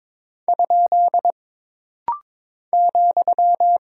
Morse code, heard as a keyed tone, about 22 wpm, 700 Hz. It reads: ? ,
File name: practice / 22wpm / Single Punctuation-Prosign - Rapid-Fire - 22wpm